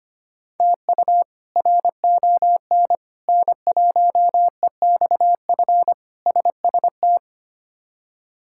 Morse code, heard as a keyed tone, about 25 wpm, 700 Hz. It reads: TU RON N1EXF SST
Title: K1USN SST QSO Practice - 25wpm